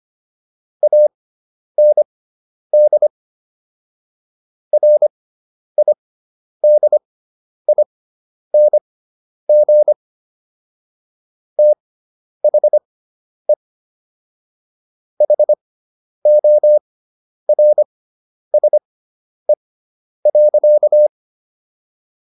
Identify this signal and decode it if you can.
Morse code, heard as a keyed tone, about 25 wpm, spaced as for 10 wpm, 600 Hz. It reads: AND RIDING THE HORSE.